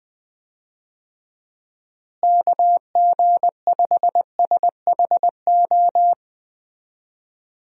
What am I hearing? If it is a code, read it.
Morse code, heard as a keyed tone, about 20 wpm, 700 Hz. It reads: KG5SHO
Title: USA Call Signs - 6 Characters - Repeated 2x - 20wpm